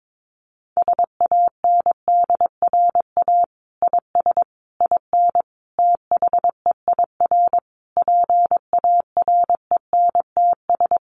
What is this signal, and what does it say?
Morse code, heard as a keyed tone, about 22 wpm, 700 Hz. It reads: SANDRA IS IN THEIR PARENTS
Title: Easy Phrases 22wpm